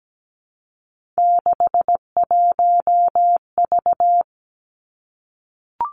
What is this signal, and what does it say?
Morse code, heard as a keyed tone, about 17 wpm, 700 Hz. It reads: 61V
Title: Three Challenging Letter-Number Combinations 17wpm